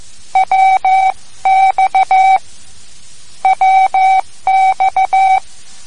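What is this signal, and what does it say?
Morse code, heard as a keyed tone, about 14 wpm, 760 Hz. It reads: WX WX
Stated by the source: Morse code lesson, third lesson in a series for learning the Morse alphabet.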